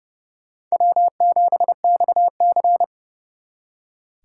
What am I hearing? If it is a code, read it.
Morse code, heard as a keyed tone, about 38 wpm, 700 Hz. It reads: W7XC